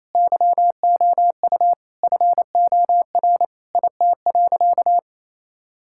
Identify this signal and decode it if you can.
Morse code, heard as a keyed tone, about 28 wpm, 700 Hz. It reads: YOU FOR IT.
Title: Sentences from Top 1000 Words - Review Part 6 - 28wpm